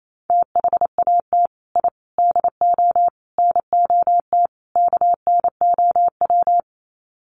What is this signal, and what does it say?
Morse code, heard as a keyed tone, about 28 wpm, 700 Hz. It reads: THAT I DO NOT KNOW
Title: Sets of 5 Words Encore2 28wpm